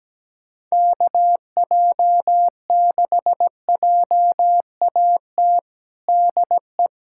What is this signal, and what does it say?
Morse code, heard as a keyed tone, about 17 wpm, 700 Hz. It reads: KJ6JAT DE
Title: USA Callsign DE Callsign - 6 Characters - 17wpm